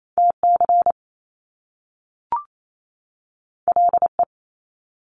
Morse code, heard as a keyed tone, about 28 wpm, 700 Hz. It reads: TC LE
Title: Top Two Letter Combinations in Words 28wpm